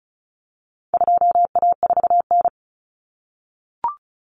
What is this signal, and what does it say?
Morse code, heard as a keyed tone, about 34 wpm, 700 Hz. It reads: 2A4N
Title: Common Cut Numbers - 4-Digit 35wpm